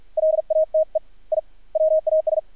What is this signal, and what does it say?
Morse code, heard as a keyed tone, about 50 wpm, 630 Hz. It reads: LATE I WAS
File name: archived / CRSnet